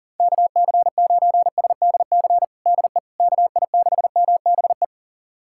Morse code, heard as a keyed tone, about 40 wpm, 700 Hz. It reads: KC9SDC DE KI6MBE
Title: USA Callsign DE Callsign - 6 Characters - 40wpm